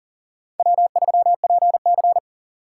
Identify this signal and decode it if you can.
Morse code, heard as a keyed tone, about 40 wpm, 700 Hz. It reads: W3PC
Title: USA Call Signs - 2x Character Spacing - 40wpm